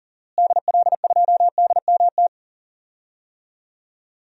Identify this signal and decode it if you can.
Morse code, heard as a keyed tone, about 40 wpm, 700 Hz. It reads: DL2DMT
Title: DXCC-100 Call Signs - 6 Characters - Repeated 2x - 40wpm